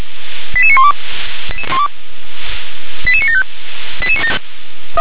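Five-tone selective call (ZVEI), heard as quick runs of five tones, tones of about 70 ms. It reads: 88912 88975
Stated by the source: ZVEI five-tone selective call received by SDR.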